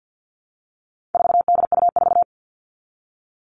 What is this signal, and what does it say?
Morse code, heard as a keyed tone, about 50 wpm, 700 Hz. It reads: 4DU4
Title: All Cut Numbers - 4-Digit 50wpm